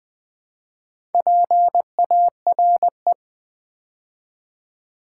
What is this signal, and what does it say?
Morse code, heard as a keyed tone, about 20 wpm, 700 Hz. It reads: PARE
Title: Build A Word - 2 Letters At A Time - 8-Letter Words - 20wpm